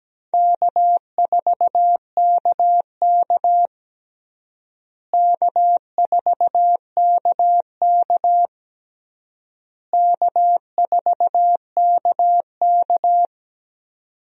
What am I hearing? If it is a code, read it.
Morse code, heard as a keyed tone, about 17 wpm, 700 Hz. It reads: K4KK K4KK K4KK
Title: USA Call Signs - 4 Characters - Repeated 3x - 17wpm